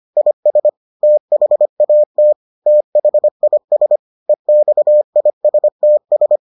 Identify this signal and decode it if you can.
Morse code, heard as a keyed tone, about 25 wpm, 600 Hz. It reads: IS THAT THIS EXISTS